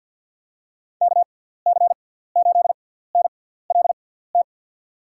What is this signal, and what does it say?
Morse code, heard as a keyed tone, about 53 wpm, spaced as for 17 wpm, 700 Hz. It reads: KC8NLT
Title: USA Call Signs - 6 Characters - 6x Character Spacing - 50wpm